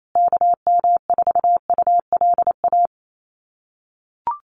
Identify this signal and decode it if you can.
Morse code, heard as a keyed tone, about 28 wpm, 700 Hz. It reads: KM4ULA
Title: USA Call Signs Repeated 2x 28wpm